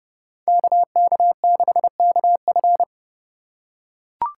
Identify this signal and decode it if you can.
Morse code, heard as a keyed tone, about 31 wpm, 700 Hz. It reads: KK6KF
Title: USA Call Signs - 5 Characters - 2x Character Spacing - 30wpm